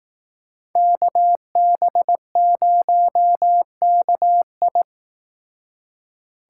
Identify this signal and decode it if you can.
Morse code, heard as a keyed tone, about 18 wpm, 700 Hz. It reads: KB0KI
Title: Speed Racing - USA Call Signs 15wpm